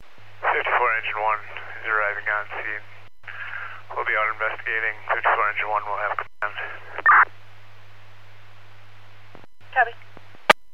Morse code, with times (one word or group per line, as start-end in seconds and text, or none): none